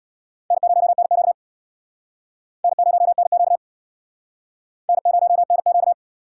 Morse code, heard as a keyed tone, about 63 wpm, 700 Hz. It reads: N0NQ N0NQ N0NQ